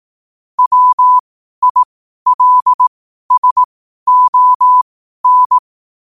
Morse code, heard as a keyed tone, about 18 wpm, spaced as for 12 wpm, 1000 Hz. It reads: WILSON
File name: qotdfiles / 12wpm